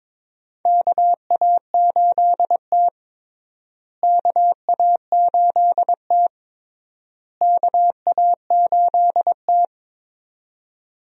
Morse code, heard as a keyed tone, about 22 wpm, 700 Hz. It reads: KA8T KA8T KA8T
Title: USA Call Signs - 4 Characters - Repeated 3x - 22wpm